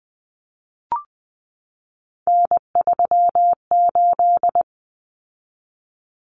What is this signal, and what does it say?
Morse code, heard as a keyed tone, about 20 wpm, 700 Hz. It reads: N38
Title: Common Cut Numbers - 3-Digit 20wpm